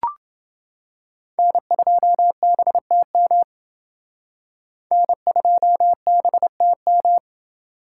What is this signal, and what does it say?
Morse code, heard as a keyed tone, about 27 wpm, 700 Hz. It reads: N2BTM N2BTM